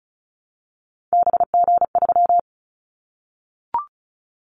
Morse code, heard as a keyed tone, about 36 wpm, 700 Hz. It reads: BG3
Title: Callsign Prefixes for Top 100 DXCC - 35wpm